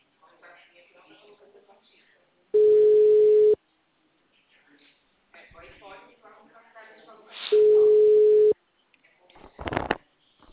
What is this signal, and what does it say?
Ringback tone (Europe, 425 Hz), with speech.